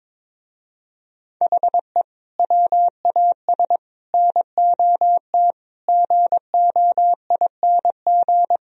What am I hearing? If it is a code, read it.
Morse code, heard as a keyed tone, about 23 wpm, 700 Hz. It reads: HE WAS NOT GOING
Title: Speed-Racing - Sets of 5 Words 22wpm